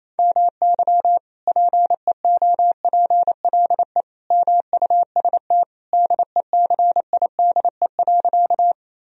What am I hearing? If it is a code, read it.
Morse code, heard as a keyed tone, about 28 wpm, 700 Hz. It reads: MY PEOPLE MUST DECIDE.
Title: Sentences from Top 1000 Words - Review Part 3 - 28wpm